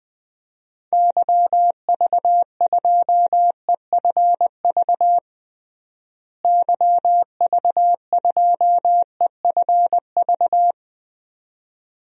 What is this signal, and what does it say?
Morse code, heard as a keyed tone, about 20 wpm, 690 Hz. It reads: YV2EFV YV2EFV
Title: DXCC-100 Call Signs - 6 Characters - Repeated 2x - 20wpm